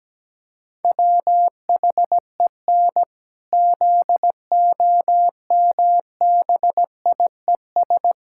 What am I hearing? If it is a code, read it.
Morse code, heard as a keyed tone, about 17 wpm, 700 Hz. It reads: WHEN ZOMBIES
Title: Pangram Challenge v2 - 17wpm